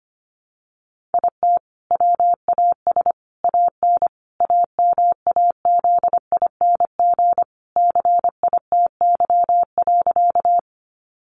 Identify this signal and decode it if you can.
Morse code, heard as a keyed tone, about 25 wpm, 700 Hz. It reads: IT WAS AN AMAZING CITY.
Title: Sentences from Top 1500 Words 25wpm